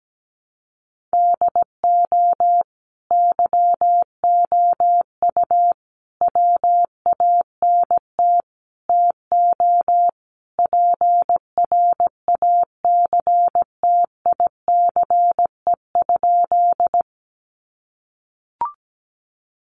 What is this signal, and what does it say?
Morse code, heard as a keyed tone, about 17 wpm, 700 Hz. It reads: DO YOU WANT TO PRACTICE?
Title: Sentences from Top 700 Words 17wpm